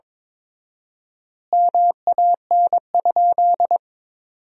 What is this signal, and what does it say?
Morse code, heard as a keyed tone, about 21 wpm, 700 Hz. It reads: MAN?
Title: 4x Word-Spacing - Sentences from Top 200 Words - 22wpm